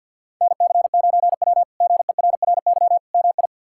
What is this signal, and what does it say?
Morse code, heard as a keyed tone, about 51 wpm, 700 Hz. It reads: NC9W GERRY MI
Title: K1USN SST QSO Practice - 50wpm